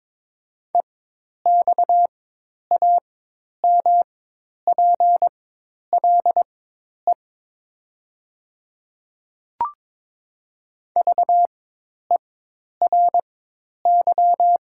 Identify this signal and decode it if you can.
Morse code, heard as a keyed tone, about 22 wpm, spaced as for 10 wpm, 700 Hz. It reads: EXAMPLE VERY
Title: 4x Character Spacing - Top 300 Words - 22wpm